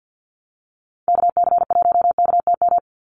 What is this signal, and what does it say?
Morse code, heard as a keyed tone, about 50 wpm, 700 Hz. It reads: KC1KTM
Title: USA Call Signs 50wpm